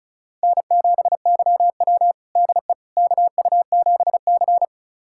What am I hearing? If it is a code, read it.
Morse code, heard as a keyed tone, about 36 wpm, 700 Hz. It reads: N7YW DE KU7C
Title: USA Callsign DE Callsign - 4 Characters - 35wpm